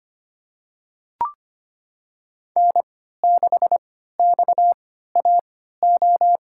Morse code, heard as a keyed tone, about 25 wpm, spaced as for 14 wpm, 700 Hz. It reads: N6XAO
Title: USA Call Signs - 3x Character Spacing - 25wpm